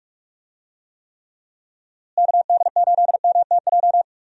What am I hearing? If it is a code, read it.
Morse code, heard as a keyed tone, about 46 wpm, 700 Hz. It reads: KD8MTJ
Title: USA Call Signs - 6 Characters - Repeated 3x - 45wpm